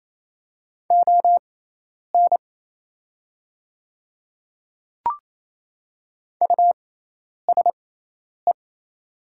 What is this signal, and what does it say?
Morse code, heard as a keyed tone, about 28 wpm, spaced as for 10 wpm, 700 Hz. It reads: ON USE
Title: 6x Character Spacing - Top 100 Words - 28wpm